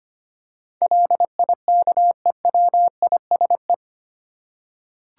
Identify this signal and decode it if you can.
Morse code, heard as a keyed tone, about 25 wpm, 700 Hz. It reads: LIKEWISE
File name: 8 Letter Words - First 3 Letters Given - 25wpm